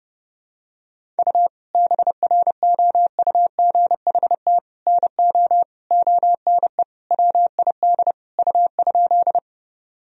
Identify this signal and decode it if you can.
Morse code, heard as a keyed tone, about 30 wpm, 700 Hz. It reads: U BROUGHT NO ONE WID U?
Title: Sentences from Top 500 Words and Abbreviations 30wpm